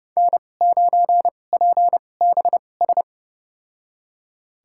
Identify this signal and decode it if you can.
Morse code, heard as a keyed tone, about 30 wpm, spaced as for 22 wpm, 700 Hz. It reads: N9PBS
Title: USA Call Signs - 5 Characters - 2x Character Spacing - 30wpm